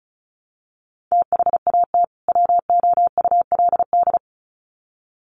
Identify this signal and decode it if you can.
Morse code, heard as a keyed tone, about 34 wpm, 700 Hz. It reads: THAT WOULD
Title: Sets of 2 Words 35wpm